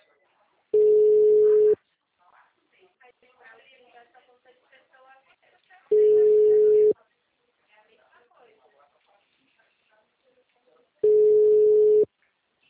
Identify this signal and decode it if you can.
Ringback tone (Europe, 425 Hz), with speech.